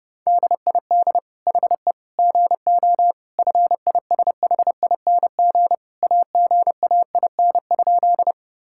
Morse code, heard as a keyed tone, about 32 wpm, 700 Hz. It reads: DID HE GO FISHING AGAIN?
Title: Sentences from Top 1000 Words - Review Part 5 - 30wpm